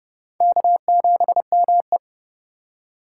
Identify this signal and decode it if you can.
Morse code, heard as a keyed tone, about 30 wpm, 700 Hz. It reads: K7ME
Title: USA Call Signs - 8x Character Spacing - 30wpm